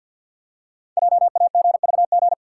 Morse code, heard as a keyed tone, about 48 wpm, 700 Hz. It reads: JAGVG
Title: Code Groups - 5 Letters - 50wpm